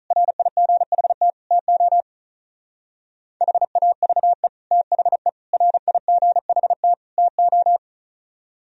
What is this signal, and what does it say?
Morse code, heard as a keyed tone, about 44 wpm, 700 Hz. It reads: RIGHT TO HAVE THE RIGHT TO